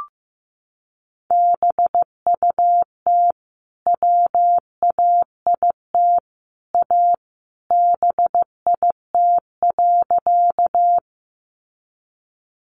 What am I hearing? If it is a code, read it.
Morse code, heard as a keyed tone, about 15 wpm, 700 Hz. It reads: BUT WAIT A BIT.
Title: Sentences from Top 900 Words 15wpm